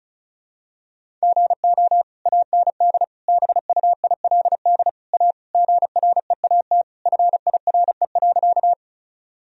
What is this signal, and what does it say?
Morse code, heard as a keyed tone, about 34 wpm, 700 Hz. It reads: GO AND BUILD A GREAT FIRE.
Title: Sentences from Top 1000 Words - Review Part 1 - 35wpm